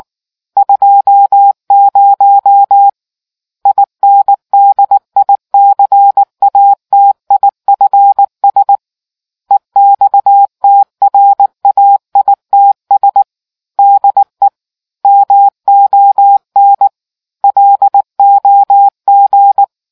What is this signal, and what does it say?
Morse code, heard as a keyed tone, about 19 wpm, 800 Hz. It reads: 20 INDICATIFS EXTRAITS DE MON LOG